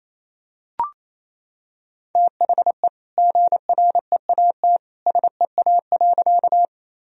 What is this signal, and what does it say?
Morse code, heard as a keyed tone, about 28 wpm, 700 Hz. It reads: THE GREAT SEA.
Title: Sentences from Top 1000 Words - Review Part 1 - 28wpm